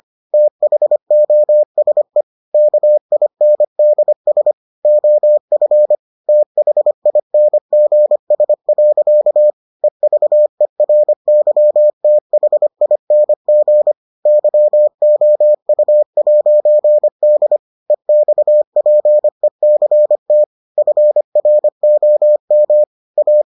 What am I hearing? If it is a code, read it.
Morse code, heard as a keyed tone, about 25 wpm, 600 Hz. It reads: THOSE KINDS OF THINGS. EVERYTHING YOU'D EXPECT FROM A